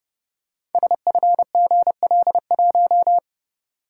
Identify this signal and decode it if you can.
Morse code, heard as a keyed tone, about 28 wpm, 700 Hz. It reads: SFGL1
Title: Code Groups - 5 Letter:Numbers - 30wpm